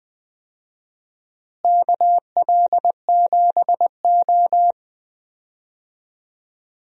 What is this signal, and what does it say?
Morse code, heard as a keyed tone, about 20 wpm, 700 Hz. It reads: KL7O
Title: DXCC-100 Call Signs - 4 Characters - Repeated 3x - 20wpm